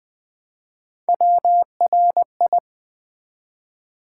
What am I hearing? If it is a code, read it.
Morse code, heard as a keyed tone, about 20 wpm, 700 Hz. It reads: WRI
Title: Build A Word - 1 Letter At A Time - 5-Letter Words - 20wpm